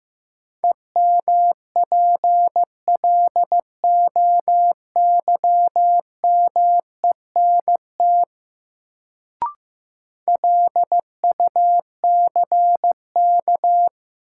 Morse code, heard as a keyed tone, about 15 wpm, 700 Hz. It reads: EMPLOYMENT LUCK
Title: Sentences from Top 1500 Words - New Word Drill - 15wpm